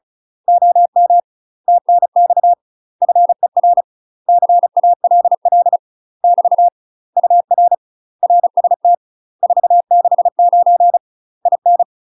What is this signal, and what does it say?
Morse code, heard as a keyed tone, about 34 wpm, 700 Hz. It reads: OM TNX FER CALL <BT> UR RST 469 IN
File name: UK MASTER QSOs TRAINING / 35WPM / UK MASTER QSOs 0022